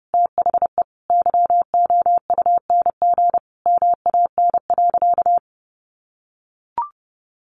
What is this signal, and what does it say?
Morse code, heard as a keyed tone, about 30 wpm, 700 Hz. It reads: THE YOUNG MAN.
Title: Sentences from Top 900 Words 30wpm